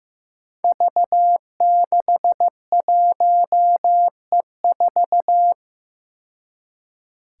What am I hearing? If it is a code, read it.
Morse code, heard as a keyed tone, about 15 wpm, 700 Hz. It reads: V61E4
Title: Five Challenging Letter-Number Combinations 15wpm